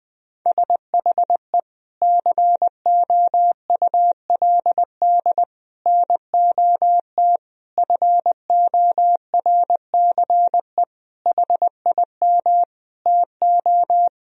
Morse code, heard as a keyed tone, about 20 wpm, 700 Hz. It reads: SHE COULD NOT FORCE HIM TO
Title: Sentences from Top 1000 Words - Review Part 3 - 20wpm